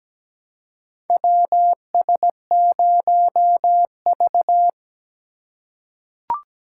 Morse code, heard as a keyed tone, about 17 wpm, 700 Hz. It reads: WS0V
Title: USA Call Signs - 3x Character Spacing - 17wpm